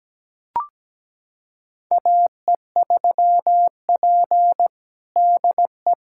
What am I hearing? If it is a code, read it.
Morse code, heard as a keyed tone, about 17 wpm, 700 Hz. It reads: AE3P DE
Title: USA Callsign DE Callsign - 4 Characters - 17wpm